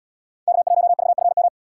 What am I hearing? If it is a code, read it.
Morse code, heard as a keyed tone, about 77 wpm, 700 Hz. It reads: K1VFM